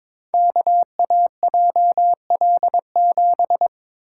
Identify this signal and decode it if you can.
Morse code, heard as a keyed tone, about 22 wpm, 700 Hz. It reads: KAJL7
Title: Code Groups - 5 Letter:Numbers - 22wpm